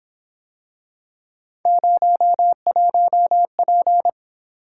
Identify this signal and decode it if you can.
Morse code, heard as a keyed tone, about 28 wpm, 700 Hz. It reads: 01P
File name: Speed-Racing - Three Letter-Number Combinations - 22wpm